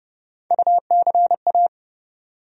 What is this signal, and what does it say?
Morse code, heard as a keyed tone, about 30 wpm, 700 Hz. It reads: UCA
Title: USA Call Sign Postfixes - 3 Letters 30wpm